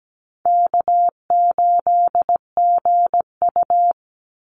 Morse code, heard as a keyed tone, about 17 wpm, 700 Hz. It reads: K8GU
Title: Call Signs - 17wpm